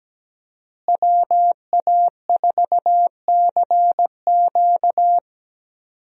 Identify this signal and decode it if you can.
Morse code, heard as a keyed tone, about 17 wpm, 700 Hz. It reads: WA4CQ
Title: USA Call Signs - 5 Characters - Repeated 2x - 17wpm